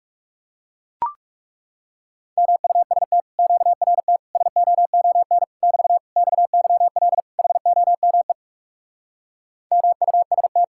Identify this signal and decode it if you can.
Morse code, heard as a keyed tone, about 40 wpm, 700 Hz. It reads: MUST QRT SOON <BT> XYL HOME MUST